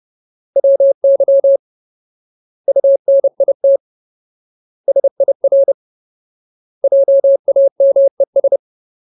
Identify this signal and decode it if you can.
Morse code, heard as a keyed tone, about 30 wpm, 550 Hz. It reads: WY UNIT SIR JAMES
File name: Ultimate Word List / 30WPM Ultimate